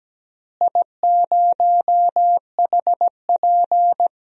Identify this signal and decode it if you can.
Morse code, heard as a keyed tone, about 17 wpm, 700 Hz. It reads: I0HP
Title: DXCC-100 Call Signs - 4 Characters - Repeated 3x - 17wpm